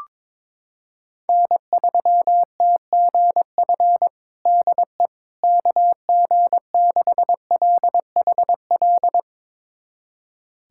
Morse code, heard as a keyed tone, about 22 wpm, 700 Hz. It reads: N3TGF DE KG6LHL
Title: USA Callsign DE Callsign 22wpm